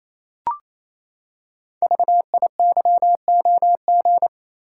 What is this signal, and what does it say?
Morse code, heard as a keyed tone, about 27 wpm, 700 Hz. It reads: VIYOG
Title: Code Groups - 5 Letters - 28wpm